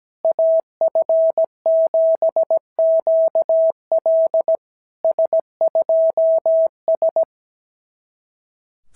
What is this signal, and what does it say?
Morse code, heard as a keyed tone, about 17 wpm, 650 Hz. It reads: AF7QL S2S
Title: SOTA QSO Practice 17wpm